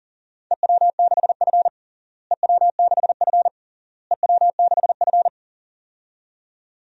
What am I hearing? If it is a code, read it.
Morse code, heard as a keyed tone, about 37 wpm, 700 Hz. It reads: EW6F EW6F EW6F
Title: DXCC-100 Call Signs - 4 Characters - Repeated 3x - 40wpm